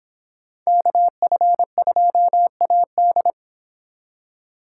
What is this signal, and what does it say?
Morse code, heard as a keyed tone, about 26 wpm, 700 Hz. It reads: KF2AD